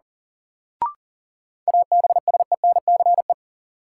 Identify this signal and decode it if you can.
Morse code, heard as a keyed tone, about 37 wpm, 700 Hz. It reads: ABSENCE
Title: Sets of 4 Words - New Word Drill - 40wpm